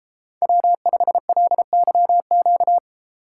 Morse code, heard as a keyed tone, about 34 wpm, 700 Hz. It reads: W5LYQ
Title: Speed Racing - USA Call Signs 28wpm